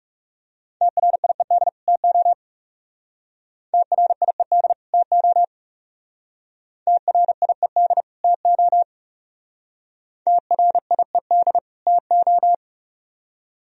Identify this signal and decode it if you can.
Morse code, heard as a keyed tone, about 36 wpm, 700 Hz. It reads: TRIED TO TRIED TO TRIED TO TRIED TO
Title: Speed-Racing - Sets of 2 Words 30wpm